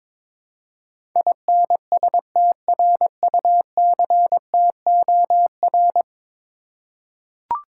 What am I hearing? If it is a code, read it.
Morse code, heard as a keyed tone, about 22 wpm, 700 Hz. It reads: INSTRUCTOR
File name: practice / 22wpm / 10 Letter Words - First 3 Letters Given - 22wpm